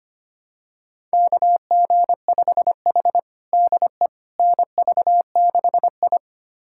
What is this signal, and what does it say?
Morse code, heard as a keyed tone, about 25 wpm, 700 Hz. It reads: KG5H DE NV6I
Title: USA Callsign DE Callsign - 4 Characters - 25wpm